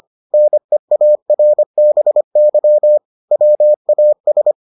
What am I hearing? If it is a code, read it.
Morse code, heard as a keyed tone, about 25 wpm, 600 Hz. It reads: NEARBY WAS